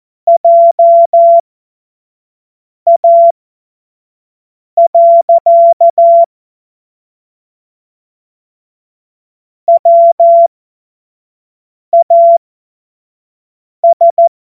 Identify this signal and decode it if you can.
Morse code, heard as a keyed tone, about 14 wpm, spaced as for 5 wpm, 680 Hz. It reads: JA. WAS
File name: Morsekurs / Witze / Witz 06. April 2021, 25 F70